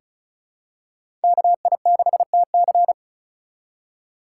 Morse code, heard as a keyed tone, about 34 wpm, 700 Hz. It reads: KI6TC
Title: USA Call Signs - 5 Characters - 2x Character Spacing - 35wpm